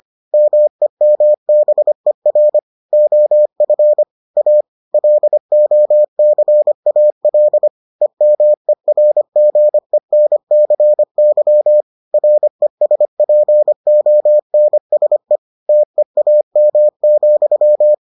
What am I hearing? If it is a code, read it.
Morse code, heard as a keyed tone, about 25 wpm, 600 Hz. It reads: MEMBER OF A LOCAL EMERGENCY RESPONSE TEAM,